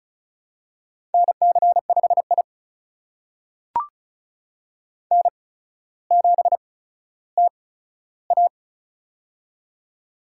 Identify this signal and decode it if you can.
Morse code, heard as a keyed tone, about 36 wpm, spaced as for 15 wpm, 700 Hz. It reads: NC5I N7TA
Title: USA Call Signs - 4 Characters - 8x Character Spacing - 35wpm